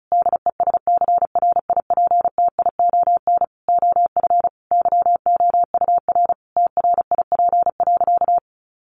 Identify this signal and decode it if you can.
Morse code, heard as a keyed tone, about 34 wpm, 700 Hz. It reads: DESCRIPTION OF YOUR TRIP.
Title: Sentences from Top 1500 Words 35wpm